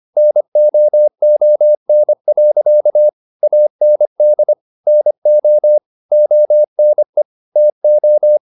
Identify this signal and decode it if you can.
Morse code, heard as a keyed tone, about 25 wpm, 600 Hz. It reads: NOON. AND NO ONE TO
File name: the old man and the sea part 1 25wpm